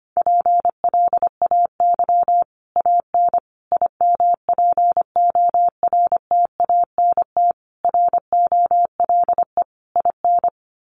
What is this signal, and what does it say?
Morse code, heard as a keyed tone, about 25 wpm, 700 Hz. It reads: PLAY AN IMPORTANT ROLE IN
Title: Sets of 5 Words Encore 25wpm